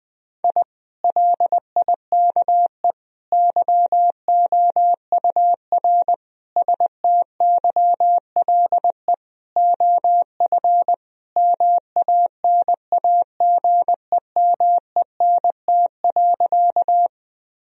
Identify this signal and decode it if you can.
Morse code, heard as a keyed tone, about 20 wpm, 700 Hz. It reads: I LIKE YOUR STYLE OF MANAGEMENT.
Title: Sentences from Top 1000 Words - Review Part 4 - 20wpm